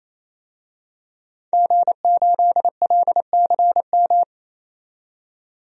Morse code, heard as a keyed tone, about 28 wpm, 700 Hz. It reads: G8LCM
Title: DXCC-100 Call Signs - 5 Characters - 28wpm